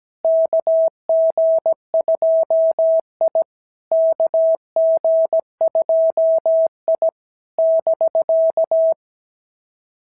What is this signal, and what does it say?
Morse code, heard as a keyed tone, about 17 wpm, 650 Hz. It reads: KG2I KG2I <BK>
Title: SOTA QSO Practice 17wpm